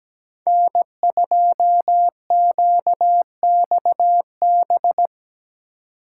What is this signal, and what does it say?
Morse code, heard as a keyed tone, about 20 wpm, 700 Hz. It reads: N2QXB
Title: Speed Racing - USA 1x3 Call Signs 17wpm